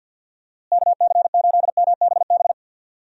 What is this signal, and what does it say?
Morse code, heard as a keyed tone, about 46 wpm, 700 Hz. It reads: KC8MBB
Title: USA Call Signs - 6 Characters - Repeated 2x - 50wpm